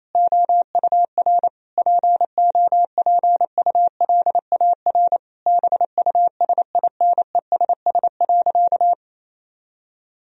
Morse code, heard as a keyed tone, about 28 wpm, 700 Hz. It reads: OUR POPULAR BUSINESS.
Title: Sentences from Top 1000 Words - Review Part 3 - 28wpm